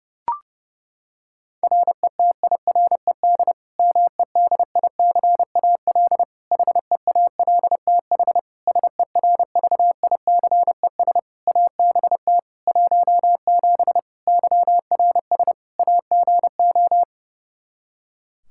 Morse code, heard as a keyed tone, about 30 wpm, 700 Hz. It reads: RETIRED MEDICAL HEALTH SERVICES ABT 17 YRS AGO
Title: K1USN SST QSO Practice 30wpm